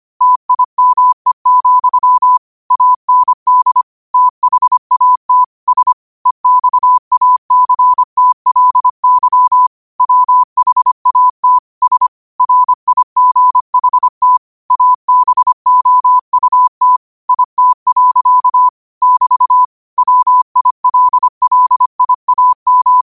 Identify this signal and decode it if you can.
Morse code, heard as a keyed tone, about 25 wpm, 1000 Hz. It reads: TIME, AND THAT S EXACTLY WHAT S RIGHT ABOUT IT. <BT> WILLIAM